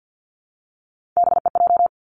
Morse code, heard as a keyed tone, about 50 wpm, 700 Hz. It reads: BEJ
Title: Three Challenging Letter-Number Combinations 50wpm